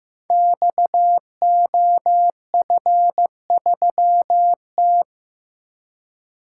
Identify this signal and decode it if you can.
Morse code, heard as a keyed tone, about 15 wpm, 700 Hz. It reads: XOF3T
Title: Code Groups - 5 Letter:Numbers - 15wpm